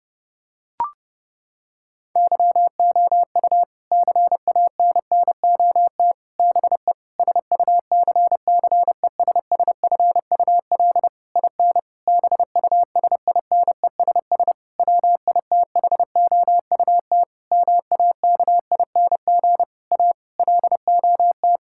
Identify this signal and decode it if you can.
Morse code, heard as a keyed tone, about 30 wpm, 700 Hz. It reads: YOU CANNOT BE SUCCESSFUL IN BUSINESS WITHOUT MAKING A LOT
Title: Sentences from Top 1000 Words - Review Part 6 - 30wpm